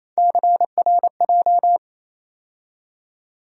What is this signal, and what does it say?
Morse code, heard as a keyed tone, about 28 wpm, 700 Hz. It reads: CRJ
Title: USA Call Sign Postfixes - 3 Letters 28wpm